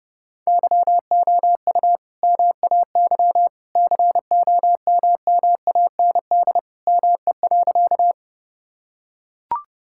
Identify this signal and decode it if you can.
Morse code, heard as a keyed tone, about 32 wpm, 700 Hz. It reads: YOU MAY COMMAND ME.
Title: Sentences from Top 1600 Words 30wpm